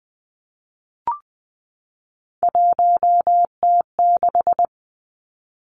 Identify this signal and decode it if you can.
Morse code, heard as a keyed tone, about 20 wpm, 700 Hz. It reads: 1T6
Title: Three Challenging Letter-Number Combinations 20wpm